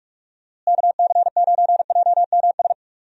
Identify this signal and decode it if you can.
Morse code, heard as a keyed tone, about 41 wpm, 700 Hz. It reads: KC9JMS
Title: USA Call Signs - 6 Characters - Repeated 2x - 45wpm